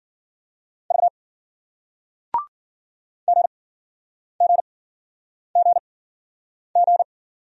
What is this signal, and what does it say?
Morse code, heard as a keyed tone, about 48 wpm, spaced as for 9 wpm, 700 Hz. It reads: V GGGG
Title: Speed-Racing Single Letters 40wpm